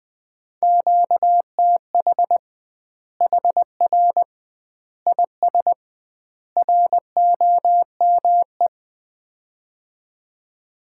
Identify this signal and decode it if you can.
Morse code, heard as a keyed tone, about 20 wpm, 700 Hz. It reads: QTH HR IS ROME
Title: Common QSO Phrases - 2x Word-Spacing - 20wpm